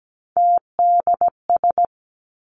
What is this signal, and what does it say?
Morse code, heard as a keyed tone, about 17 wpm, 700 Hz. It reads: TDS
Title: Three Challenging Letter-Number Combinations 17wpm